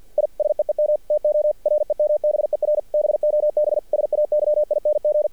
Morse code, heard as a keyed tone, about 50 wpm, 600 Hz. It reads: I SEEM TO REMBER BOB SAYING
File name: archived / CRSnet